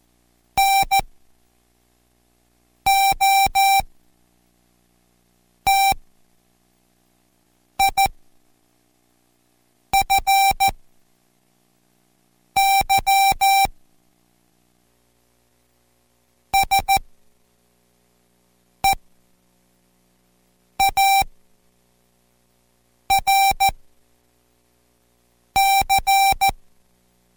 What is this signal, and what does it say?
Morse code, heard as a keyed tone, about 14 wpm, spaced as for 4 wpm, 780 Hz. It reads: NOTIFY SEARC